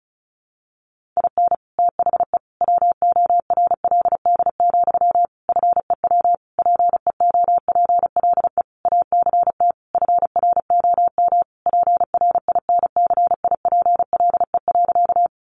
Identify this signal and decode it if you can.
Morse code, heard as a keyed tone, about 34 wpm, 700 Hz. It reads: IN THE WORLD, FEW PEOPLE ACT FROM PRINCIPLE.
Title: Sentences from Top 1400 Words 35wpm